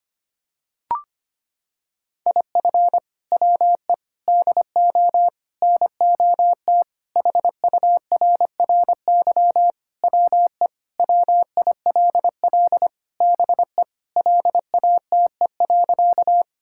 Morse code, heard as a keyed tone, about 25 wpm, 700 Hz. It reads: IF WE DO NOT HURRY WE WILL BE LATE.
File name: practice / 25wpm / Sentences with 4-Word Phrases - 25wpm